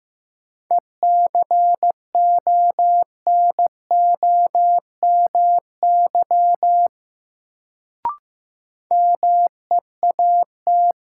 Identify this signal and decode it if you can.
Morse code, heard as a keyed tone, about 15 wpm, 700 Hz. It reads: ECONOMY MEAT
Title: Sentences from Top 1000 Words - Review Part 3 - New Word Drill - 15wpm